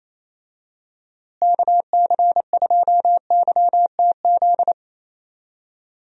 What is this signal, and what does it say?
Morse code, heard as a keyed tone, about 28 wpm, 700 Hz. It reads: KC2YTZ
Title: USA Call Signs - 6 Characters - Repeated 2x - 28wpm